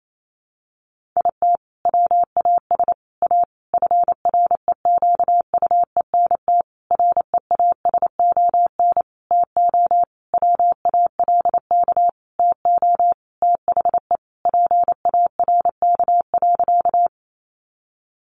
Morse code, heard as a keyed tone, about 28 wpm, 700 Hz. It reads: IT WAS A FREQUENT REASON TO WALK TO THE PARK.
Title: Sentences from Top 1400 Words 28wpm